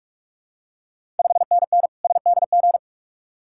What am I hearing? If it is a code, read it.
Morse code, heard as a keyed tone, about 44 wpm, 700 Hz. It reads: 5NN SDG
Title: Ontario QSO Party Exchange 45wpm